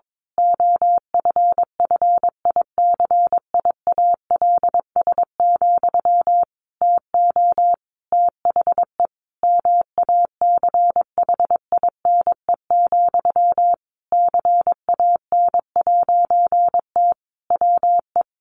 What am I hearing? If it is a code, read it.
Morse code, heard as a keyed tone, about 22 wpm, 700 Hz. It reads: OFFICIALS, TO THE MACHINE, CAN'T WE
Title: A Sound Of Thunder 22wpm